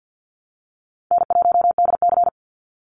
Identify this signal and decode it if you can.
Morse code, heard as a keyed tone, about 50 wpm, 700 Hz. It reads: N1DC
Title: Call Signs - 50wpm